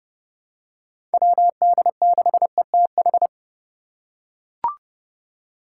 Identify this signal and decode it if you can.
Morse code, heard as a keyed tone, about 30 wpm, 700 Hz. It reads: WD6ETH
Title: USA Call Signs - 6 Characters - 3x Character Spacing - 30wpm